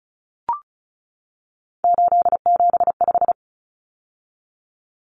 Morse code, heard as a keyed tone, about 34 wpm, 700 Hz. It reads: 875
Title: Numbers - 3-Digit 35wpm